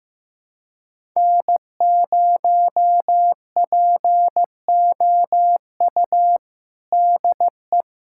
Morse code, heard as a keyed tone, about 15 wpm, 700 Hz. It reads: N0POU DE
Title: USA Callsign DE Callsign - 5 Characters - 15wpm